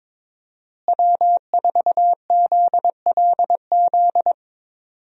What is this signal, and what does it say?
Morse code, heard as a keyed tone, about 22 wpm, 700 Hz. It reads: W4ZLZ
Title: USA Call Signs - 5 Characters - Repeated 3x - 22wpm